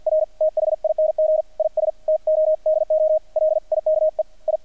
Morse code, heard as a keyed tone, about 50 wpm, 620 Hz, with radio noise. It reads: R THING IS TO DO LIKE I
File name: archived / CRSnet